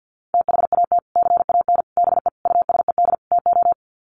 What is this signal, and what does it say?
Morse code, heard as a keyed tone, about 50 wpm, 700 Hz. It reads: THAT CAN BE USED TO
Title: Sets of 5 Words Encore 50wpm